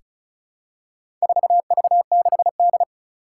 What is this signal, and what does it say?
Morse code, heard as a keyed tone, about 36 wpm, 700 Hz. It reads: 4V6D
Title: Four Challenging Letter-Number Combinations 35wpm